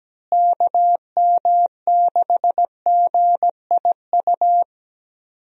Morse code, heard as a keyed tone, about 17 wpm, 700 Hz. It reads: KM6GIU
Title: USA Call Signs - 6 Characters - Repeated 2x - 17wpm